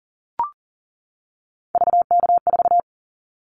Morse code, heard as a keyed tone, about 42 wpm, 700 Hz. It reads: VK4
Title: Callsign Prefixes for Top 50 DXCC - 40wpm